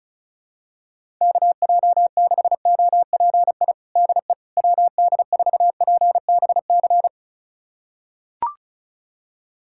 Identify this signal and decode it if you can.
Morse code, heard as a keyed tone, about 34 wpm, 700 Hz. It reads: KJ6OPI DE WD4PBC
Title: USA Callsign DE Callsign - 6 Characters - 35wpm